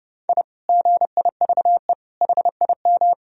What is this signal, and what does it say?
Morse code, heard as a keyed tone, about 30 wpm, 700 Hz. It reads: I GIVE HIM
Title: Sentences from Top 1000 Words - Review Part 3 - 30wpm